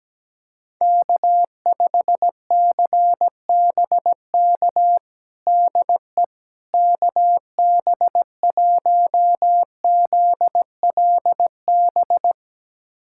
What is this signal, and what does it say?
Morse code, heard as a keyed tone, about 17 wpm, 700 Hz. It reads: K5CBK DE KB1ZLB
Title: USA Callsign DE Callsign 17wpm